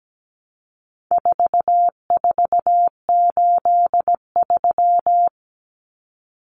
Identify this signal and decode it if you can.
Morse code, heard as a keyed tone, about 17 wpm, 700 Hz. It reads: 4483
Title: Common Cut Numbers - 4-Digit 17wpm